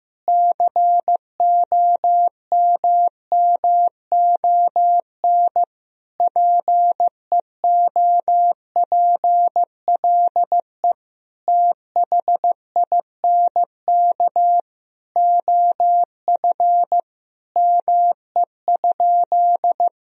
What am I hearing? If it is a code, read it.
Morse code, heard as a keyed tone, about 15 wpm, 700 Hz. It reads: COMMON PEOPLE THINK OF ME?
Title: Sentences from Top 1000 Words - Review Part 3 - 15wpm